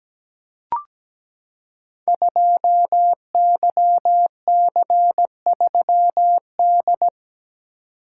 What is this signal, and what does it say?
Morse code, heard as a keyed tone, about 17 wpm, 700 Hz. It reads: 2YC3D
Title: Code Groups - 5 Letter:Numbers - 17wpm